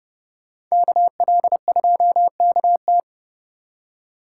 Morse code, heard as a keyed tone, about 38 wpm, 700 Hz. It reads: KL2KT